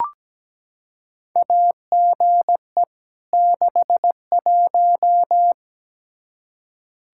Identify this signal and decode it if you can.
Morse code, heard as a keyed tone, about 17 wpm, 700 Hz. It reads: AGE 61
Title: K1USN SST QSO Practice 17wpm